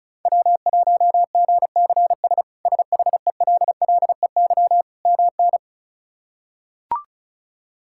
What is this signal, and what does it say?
Morse code, heard as a keyed tone, about 36 wpm, 700 Hz. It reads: W1GCS SHELLEY MN
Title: K1USN SST QSO Practice - 35wpm